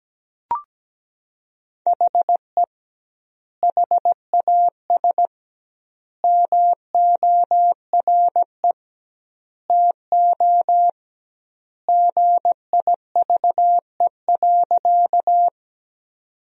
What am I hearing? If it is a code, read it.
Morse code, heard as a keyed tone, about 17 wpm, 700 Hz. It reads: HE HAS MORE TO GIVE.
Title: 2x Word-Spacing - Sentences from Top 200 Words - 17wpm